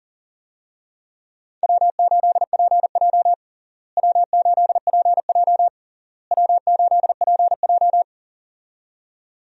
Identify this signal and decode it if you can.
Morse code, heard as a keyed tone, about 37 wpm, 700 Hz. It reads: W8PJ W8PJ W8PJ
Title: USA Call Signs - 4 Characters - Repeated 3x - 40wpm